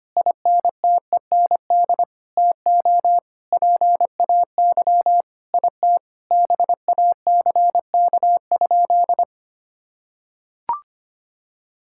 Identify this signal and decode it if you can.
Morse code, heard as a keyed tone, about 25 wpm, 700 Hz. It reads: INTEND TO PAY IT BACK?
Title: Sentences from Top 1600 Words 25wpm